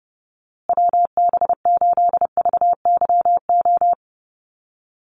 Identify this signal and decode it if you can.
Morse code, heard as a keyed tone, about 30 wpm, 700 Hz. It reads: WB8VYO
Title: USA Call Signs Repeated 3x 30wpm